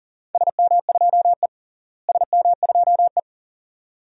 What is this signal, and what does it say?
Morse code, heard as a keyed tone, about 40 wpm, 700 Hz. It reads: SM2E SM2E
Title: DXCC-100 Call Signs - 4 Characters - Repeated 2x - 40wpm